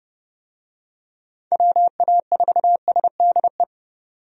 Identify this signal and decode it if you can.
Morse code, heard as a keyed tone, about 30 wpm, 700 Hz. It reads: WA4SDE
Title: USA Call Signs - 6 Characters - 30wpm